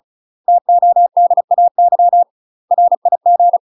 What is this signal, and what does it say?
Morse code, heard as a keyed tone, about 36 wpm, 700 Hz. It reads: TODAY RIG
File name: UK MASTER QSOs TRAINING / 35WPM / UK MASTER QSOs 0011